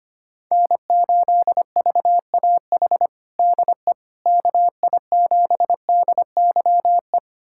Morse code, heard as a keyed tone, about 25 wpm, 700 Hz. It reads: N8VAH DE KI7DYE
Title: USA Callsign DE Callsign 25wpm